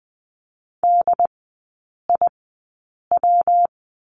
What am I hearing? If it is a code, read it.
Morse code, heard as a keyed tone, about 20 wpm, spaced as for 8 wpm, 700 Hz. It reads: DIW
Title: 133 Lesson 22e - Question Mark - Three Characters 20wpm